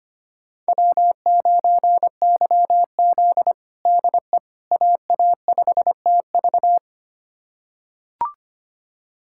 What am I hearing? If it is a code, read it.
Morse code, heard as a keyed tone, about 25 wpm, 700 Hz. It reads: W9YZ DE AA5TV
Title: USA Callsign DE Callsign 25wpm